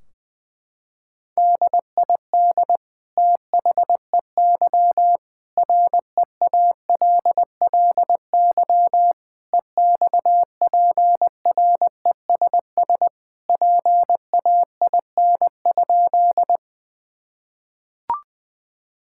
Morse code, heard as a keyed tone, about 20 wpm, 700 Hz. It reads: DID THEY REALLY EXPRESS PAIN?
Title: Sentences from Top 1600 Words 20wpm